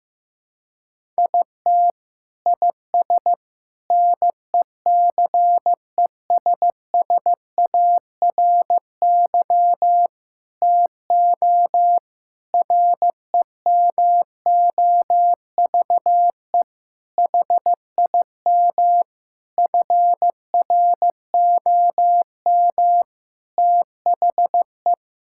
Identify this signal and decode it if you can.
Morse code, heard as a keyed tone, about 16 wpm, 700 Hz. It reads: IT IS NECESSARY TO REMOVE HIM FROM THE
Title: Sentences from Top 1000 Words - Review Part 5 - 15wpm